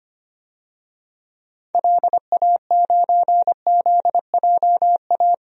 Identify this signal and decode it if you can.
Morse code, heard as a keyed tone, about 25 wpm, 700 Hz. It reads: LA9ZJA
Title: DXCC-100 Call Signs - 6 Characters - 25wpm